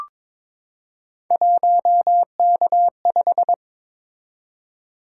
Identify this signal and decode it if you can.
Morse code, heard as a keyed tone, about 22 wpm, 700 Hz. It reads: 1K5
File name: practice / 22wpm / Three Letter-Number Combinations - 22wpm